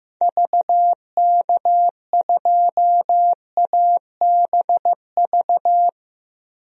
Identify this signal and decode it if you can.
Morse code, heard as a keyed tone, about 15 wpm, 700 Hz. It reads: VK2ABV
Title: DXCC-100 Call Signs - 6 Characters - 15wpm